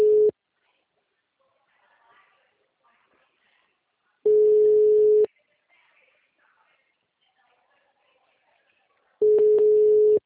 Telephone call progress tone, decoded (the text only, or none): ringback tone